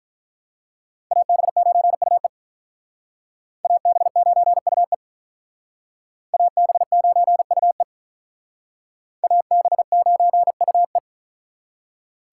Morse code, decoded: AB9UE AB9UE AB9UE AB9UE

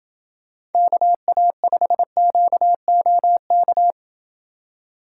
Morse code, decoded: KA5QOK

